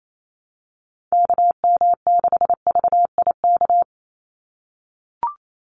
Morse code, heard as a keyed tone, about 28 wpm, 700 Hz. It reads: KM6VIK
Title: USA Call Signs 28wpm